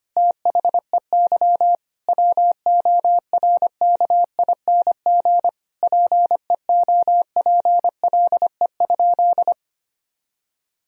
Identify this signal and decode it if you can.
Morse code, heard as a keyed tone, about 25 wpm, 700 Hz. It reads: THEY WORKING PEOPLE?
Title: Sentences from Top 1000 Words - Review Part 4 - 25wpm